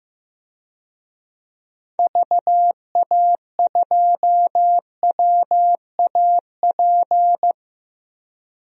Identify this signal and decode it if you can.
Morse code, heard as a keyed tone, about 15 wpm, 700 Hz. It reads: VA2WAP
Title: DXCC-100 Call Signs - 6 Characters - Repeated 2x - 15wpm